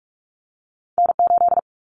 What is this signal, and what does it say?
Morse code, heard as a keyed tone, about 46 wpm, 700 Hz. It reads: N8
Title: Common Cut Numbers - 2-Digit 45wpm